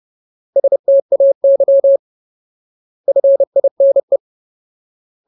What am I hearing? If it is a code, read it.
Morse code, heard as a keyed tone, about 30 wpm, 550 Hz. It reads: STAY FINE
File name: Ultimate Word List / 30WPM Ultimate